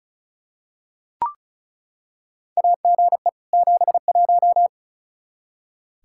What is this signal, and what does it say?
Morse code, heard as a keyed tone, about 36 wpm, 700 Hz. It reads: AGE 71
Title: Practice QSO - Exchange 3 - More Info - 35wpm